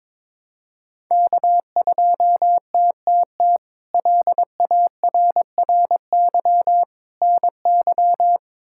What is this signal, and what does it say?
Morse code, heard as a keyed tone, about 22 wpm, 700 Hz. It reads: K2TTT LARRY NY
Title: K1USN SST QSO Practice - 22wpm